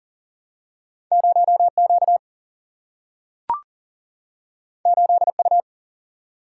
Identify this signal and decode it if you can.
Morse code, heard as a keyed tone, about 40 wpm, 700 Hz. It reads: 0Q 8U
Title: Single-Letter Single-Number 40wpm